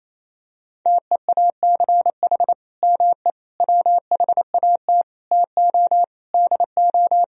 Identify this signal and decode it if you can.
Morse code, heard as a keyed tone, about 28 wpm, 700 Hz. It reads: TEACH ME WHAT TO DO
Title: Sentences from Top 500 Words and Abbreviations 28wpm